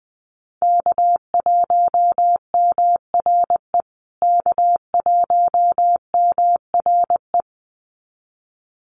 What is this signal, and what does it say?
Morse code, heard as a keyed tone, about 20 wpm, 700 Hz. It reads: K1MRE K1MRE
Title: USA Call Signs Repeated 2x 20wpm